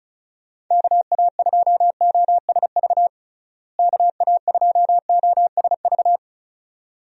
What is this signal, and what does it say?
Morse code, heard as a keyed tone, about 34 wpm, 700 Hz. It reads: KA2OSV KA2OSV
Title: USA Call Signs - 6 Characters - Repeated 2x - 35wpm